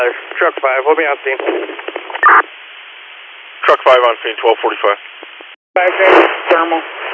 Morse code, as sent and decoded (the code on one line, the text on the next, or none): none